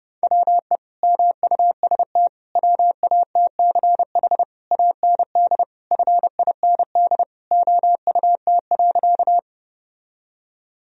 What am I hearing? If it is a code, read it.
Morse code, heard as a keyed tone, about 30 wpm, 700 Hz. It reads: WE MUST WATCH AND FIND OUT.
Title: Sentences from Top 1000 Words - Review Part 1 - 30wpm